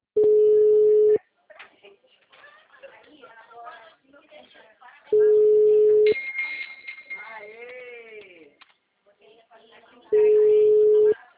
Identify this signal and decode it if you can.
Ringback tone (Europe, 425 Hz), with speech.